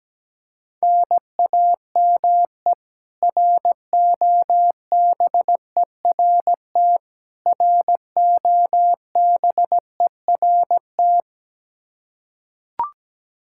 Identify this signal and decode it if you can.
Morse code, heard as a keyed tone, about 17 wpm, 700 Hz. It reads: NAME ROBERT ROBERT
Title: Practice QSO - Exchange 1 - Beginning - 17wpm